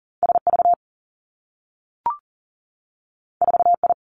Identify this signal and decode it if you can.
Morse code, heard as a keyed tone, about 39 wpm, 700 Hz. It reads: S V 4 I
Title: Two Challenging Letter-Number Combinations 40wpm